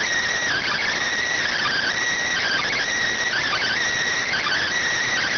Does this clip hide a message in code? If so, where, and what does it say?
Yes: Emergency Alert System (SAME) tones from 0.5 to 4.6 s, heard as harsh buzzing bursts.